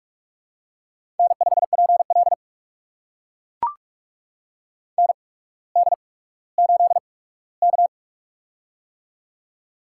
Morse code, heard as a keyed tone, about 44 wpm, spaced as for 19 wpm, 700 Hz. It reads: N5PL ND8K
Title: USA Call Signs - 4 Characters - 8x Character Spacing - 45wpm